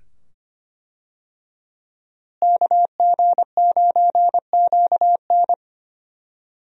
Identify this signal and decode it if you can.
Morse code, heard as a keyed tone, about 25 wpm, 700 Hz. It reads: KG9QN
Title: USA Call Signs - 5 Characters - Repeated 2x - 25wpm